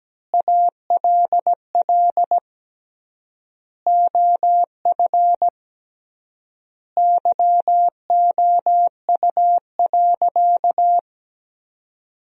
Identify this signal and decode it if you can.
Morse code, heard as a keyed tone, about 17 wpm, 700 Hz. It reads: ALL OF YOU.
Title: 3x Word-Spacing - Sentences from Top 500 Words - 17wpm